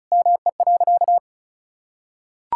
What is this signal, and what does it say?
Morse code, heard as a keyed tone, about 36 wpm, 700 Hz. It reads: ME.
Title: Sentences from Top 1600 Words 35wpm